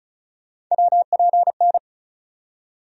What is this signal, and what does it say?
Morse code, heard as a keyed tone, about 36 wpm, 700 Hz. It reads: WPN
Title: USA Call Sign Postfixes - 3 Letters 35wpm